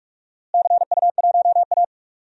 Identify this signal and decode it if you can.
Morse code, heard as a keyed tone, about 46 wpm, 700 Hz. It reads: CU1A
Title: DXCC-100 Call Signs - 4 Characters - Repeated 3x - 45wpm